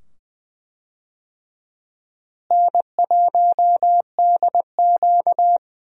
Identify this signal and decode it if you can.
Morse code, heard as a keyed tone, about 20 wpm, 700 Hz. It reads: N1DQ
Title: USA Call Signs - 4 Characters - Repeated 2x - 20wpm